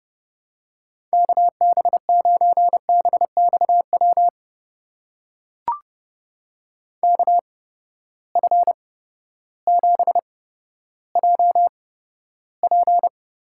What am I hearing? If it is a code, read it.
Morse code, heard as a keyed tone, about 32 wpm, spaced as for 14 wpm, 700 Hz. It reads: KB9BXW KF7JP
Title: USA Call Signs - 6 Characters - 8x Character Spacing - 30wpm